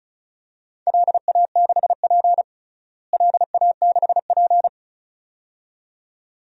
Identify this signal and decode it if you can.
Morse code, heard as a keyed tone, about 34 wpm, 700 Hz. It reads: LA6P LA6P
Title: DXCC-100 Call Signs - 4 Characters - Repeated 2x - 35wpm